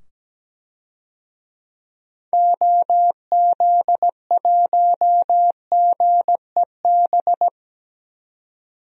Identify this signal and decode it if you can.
Morse code, heard as a keyed tone, about 17 wpm, 700 Hz. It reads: OZ1GEB
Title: DXCC-100 Call Signs - 6 Characters - 17wpm